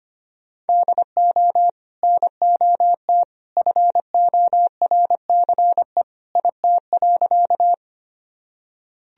Morse code, heard as a keyed tone, about 25 wpm, 700 Hz. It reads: DO NOT FORCE IT.
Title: Sentences from Top 1000 Words - Review Part 3 - 25wpm